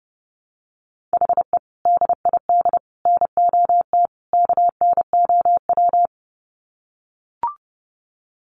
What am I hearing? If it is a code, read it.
Morse code, heard as a keyed tone, about 30 wpm, 700 Hz. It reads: HE DID NOT KNOW
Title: Sets of 4 Words 30wpm